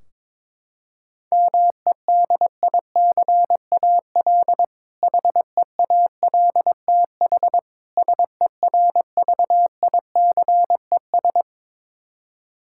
Morse code, heard as a keyed tone, about 22 wpm, 700 Hz. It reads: MEDICAL HEALTH SERVICES
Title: K1USN SST QSO Practice 22wpm